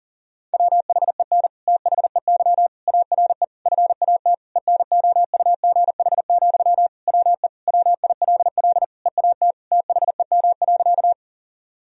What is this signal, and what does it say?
Morse code, heard as a keyed tone, about 37 wpm, 700 Hz. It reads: WHEN THEY ARE FAT ENOUGH, WE WILL EAT THEM.
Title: Sentences from Top 1000 Words - Review Part 4 - 40wpm